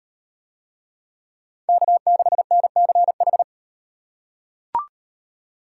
Morse code, heard as a keyed tone, about 38 wpm, 700 Hz. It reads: K6NCH